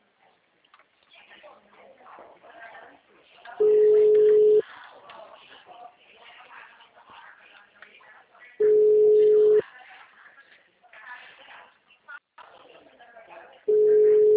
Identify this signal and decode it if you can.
Ringback tone (Europe, 425 Hz), with speech.